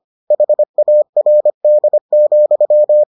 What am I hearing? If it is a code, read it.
Morse code, heard as a keyed tone, about 25 wpm, 600 Hz. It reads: HARD,